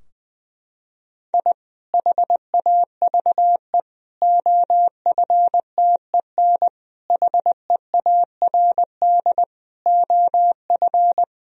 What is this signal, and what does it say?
Morse code, heard as a keyed tone, about 20 wpm, 700 Hz. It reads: I HAVE OFTEN HEARD OF
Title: Sentences from Top 1000 Words - Review Part 2 - 20wpm